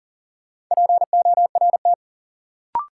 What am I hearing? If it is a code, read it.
Morse code, heard as a keyed tone, about 40 wpm, 700 Hz. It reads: PORT
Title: Sentences from Top 1000 Words - Review Part 1 - New Word Drill - 40wpm